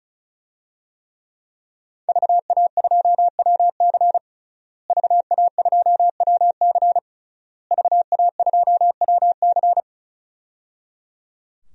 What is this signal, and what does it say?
Morse code, heard as a keyed tone, about 36 wpm, 700 Hz. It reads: VA2WC VA2WC VA2WC
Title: DXCC-100 Call Signs - 5 Characters - Repeated 3x - 35wpm